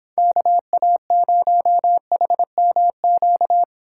Morse code, heard as a keyed tone, about 24 wpm, 700 Hz. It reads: KA0HMQ